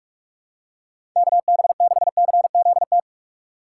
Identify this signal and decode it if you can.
Morse code, heard as a keyed tone, about 43 wpm, 700 Hz. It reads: KB6CZT
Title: USA Call Signs - 6 Characters - 3x Character Spacing - 45wpm